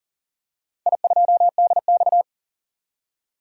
Morse code, heard as a keyed tone, about 40 wpm, 700 Hz. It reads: I2DX
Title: DXCC-100 Call Signs - 4 Characters - 40wpm